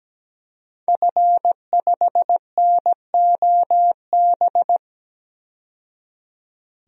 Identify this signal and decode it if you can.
Morse code, heard as a keyed tone, about 17 wpm, 700 Hz. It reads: F5NOB
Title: DXCC-100 Call Signs - 5 Characters - 17wpm